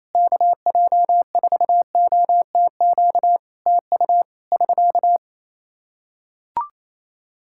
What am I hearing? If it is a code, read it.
Morse code, heard as a keyed tone, about 27 wpm, 700 Hz. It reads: KJ4OTQ TU <SK>
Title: Practice QSO - Exchange 4 - Ending - 28wpm